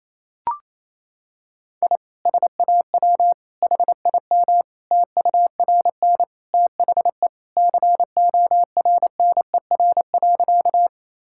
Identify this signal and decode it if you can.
Morse code, heard as a keyed tone, about 28 wpm, 700 Hz. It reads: I SAW HIM TURN THE CORNER.
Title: Sentences from Top 1600 Words 28wpm